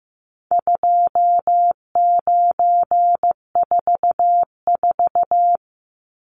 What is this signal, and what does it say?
Morse code, heard as a keyed tone, about 15 wpm, 700 Hz. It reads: 2944
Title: Numbers - 4-Digit 15wpm